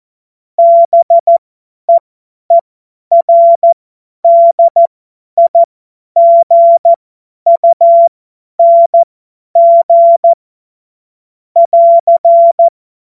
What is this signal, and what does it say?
Morse code, heard as a keyed tone, about 14 wpm, spaced as for 10 wpm, 680 Hz. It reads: BEERDIGUNG <AR>
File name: Morsekurs / Witze / Witz 16. Februar 2021, 50 F70